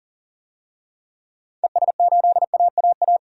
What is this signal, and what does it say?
Morse code, heard as a keyed tone, about 40 wpm, 700 Hz. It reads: ES8AAA
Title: DXCC-100 Call Signs - 6 Characters - 40wpm